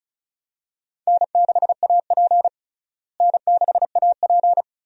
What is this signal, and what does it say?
Morse code, heard as a keyed tone, about 34 wpm, 700 Hz. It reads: N6AP N6AP
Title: USA Call Signs - 4 Characters - Repeated 2x - 35wpm